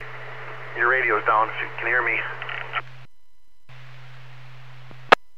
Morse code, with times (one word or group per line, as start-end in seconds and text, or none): none